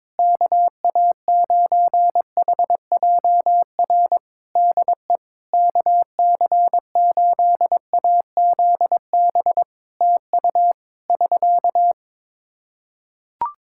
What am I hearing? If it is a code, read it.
Morse code, heard as a keyed tone, about 22 wpm, 700 Hz. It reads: KA9HJR DE KC8AZB TU <SK>
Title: Practice QSO - Exchange 4 - Ending - 22wpm